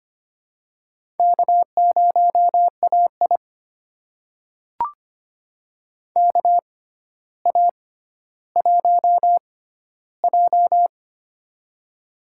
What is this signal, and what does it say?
Morse code, heard as a keyed tone, about 25 wpm, spaced as for 13 wpm, 700 Hz. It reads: K0AI KA1J
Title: USA Call Signs - 4 Characters - 6x Character Spacing - 25wpm